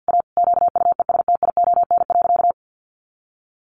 Morse code, heard as a keyed tone, about 50 wpm, 700 Hz. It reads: A QUESTION.
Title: Sentences from Top 400 Words 50wpm